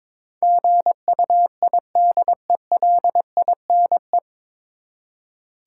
Morse code, heard as a keyed tone, about 22 wpm, 700 Hz. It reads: GUIDELINE
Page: Morse Code Ninja practice 22wpm